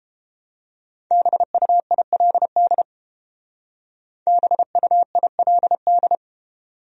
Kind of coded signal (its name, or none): Morse code